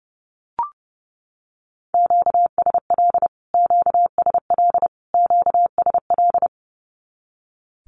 Morse code, heard as a keyed tone, about 30 wpm, 700 Hz. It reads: QSL QSL QSL
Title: Q-Codes Repeated 3x 30wpm